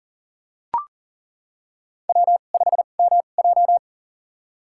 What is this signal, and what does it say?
Morse code, heard as a keyed tone, about 40 wpm, spaced as for 29 wpm, 700 Hz. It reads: W5MJ
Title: USA Call Signs - 4 Characters - 2x Character Spacing - 40wpm